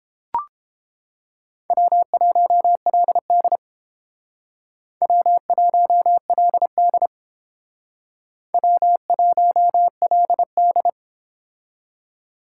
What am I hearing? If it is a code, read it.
Morse code, heard as a keyed tone, about 30 wpm, 700 Hz. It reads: W1LD W1LD W1LD